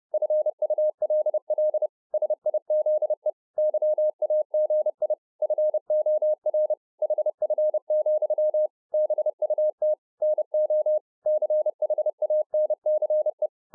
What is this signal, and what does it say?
Morse code, heard as a keyed tone, about 30 wpm, 610 Hz. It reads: FULL SIZE YAGI FOR HF, BUT NO CHANCE